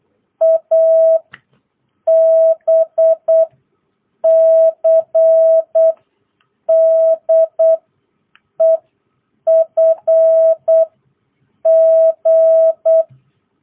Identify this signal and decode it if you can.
Morse code, heard as a keyed tone, about 8 wpm, 650 Hz. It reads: A B C D E F G
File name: jh8cbh / 2022cw